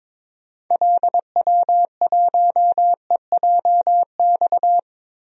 Morse code, heard as a keyed tone, about 22 wpm, 700 Hz. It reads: LW1EJX
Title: DXCC-100 Call Signs - 6 Characters - Repeated 2x - 22wpm